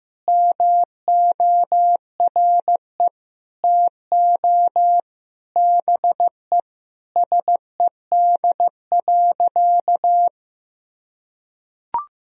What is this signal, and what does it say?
Morse code, heard as a keyed tone, about 15 wpm, 700 Hz. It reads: MORE TO BE SED.
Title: Sentences from Top 100 Words and Abbreviations 15wpm